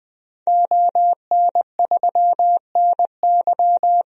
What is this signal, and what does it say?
Morse code, heard as a keyed tone, about 20 wpm, 700 Hz. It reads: ON3NY
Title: DXCC-100 Call Signs - 5 Characters - Repeated 3x - 20wpm